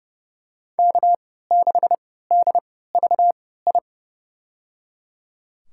Morse code, heard as a keyed tone, about 30 wpm, spaced as for 17 wpm, 700 Hz. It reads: K6DVI
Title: USA Call Signs - 3x Character Spacing - 30wpm